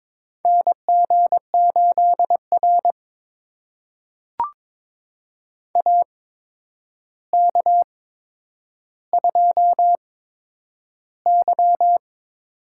Morse code, heard as a keyed tone, about 21 wpm, spaced as for 9 wpm, 700 Hz. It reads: NG8R AK2Y
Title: USA Call Signs - 4 Characters - 8x Character Spacing - 22wpm